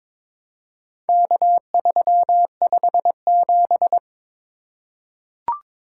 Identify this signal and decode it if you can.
Morse code, heard as a keyed tone, about 21 wpm, 700 Hz. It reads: K357
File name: practice / 22wpm / Four Letter-Number Combinations - 22wpm